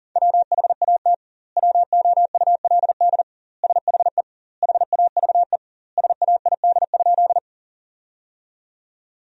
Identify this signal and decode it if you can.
Morse code, heard as a keyed tone, about 40 wpm, 700 Hz. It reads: WHAT WOULD SHE HAVE SAID?
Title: 2x Word-Spacing - Sentences from Top 100 Words - 40wpm